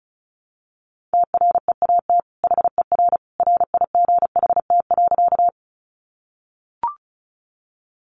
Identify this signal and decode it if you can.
Morse code, heard as a keyed tone, about 34 wpm, 700 Hz. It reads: TREAT HER RIGHT.
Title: Sentences from Top 1300 Words 35wpm